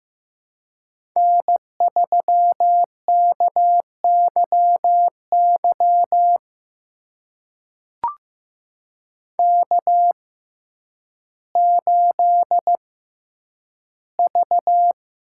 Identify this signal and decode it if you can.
Morse code, heard as a keyed tone, about 15 wpm, spaced as for 9 wpm, 700 Hz. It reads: N3KYY K8V
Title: USA Call Signs - 6x Character Spacing - 15wpm